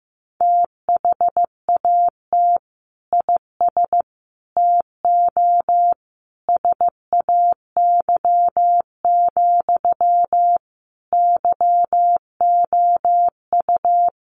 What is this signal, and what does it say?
Morse code, heard as a keyed tone, about 15 wpm, 700 Hz. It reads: THAT IS TO SAY, YOU
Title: Sentences from Top 700 Words 15wpm